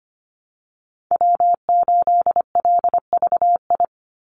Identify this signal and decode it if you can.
Morse code, heard as a keyed tone, about 25 wpm, 700 Hz. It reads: W8LVI
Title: Call Signs Repeated 2x - 25wpm